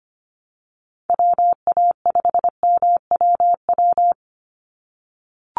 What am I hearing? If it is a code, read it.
Morse code, heard as a keyed tone, about 25 wpm, 700 Hz. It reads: WA5MWW
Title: USA Call Signs 25wpm